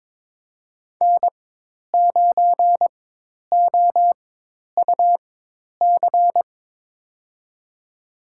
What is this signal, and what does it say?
Morse code, heard as a keyed tone, about 22 wpm, spaced as for 10 wpm, 700 Hz. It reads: N9OUC
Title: USA Call Signs - 5 Characters - 4x Character Spacing - 22wpm